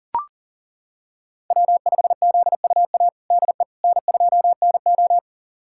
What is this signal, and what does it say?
Morse code, heard as a keyed tone, about 40 wpm, 700 Hz. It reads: W5ZUA DE N2NO
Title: USA Callsign DE Callsign 40wpm